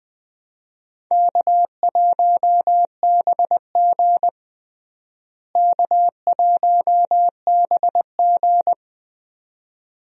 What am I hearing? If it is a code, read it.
Morse code, heard as a keyed tone, about 21 wpm, 700 Hz. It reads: K1BG K1BG
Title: USA Call Signs - 4 Characters - Repeated 2x - 20wpm